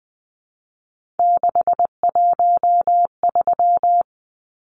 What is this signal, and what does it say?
Morse code, heard as a keyed tone, about 20 wpm, 700 Hz. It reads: 613
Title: Numbers - 3-Digit 20wpm